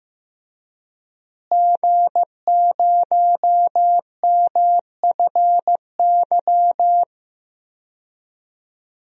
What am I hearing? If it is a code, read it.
Morse code, heard as a keyed tone, about 15 wpm, 700 Hz. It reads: G0MFY
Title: DXCC-100 Call Signs - 5 Characters - Repeated 3x - 15wpm